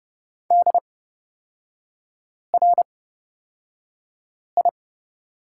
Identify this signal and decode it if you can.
Morse code, heard as a keyed tone, about 29 wpm, spaced as for 5 wpm, 700 Hz. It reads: DRI
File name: Lesson 36e - BT - Three Characters 30wpm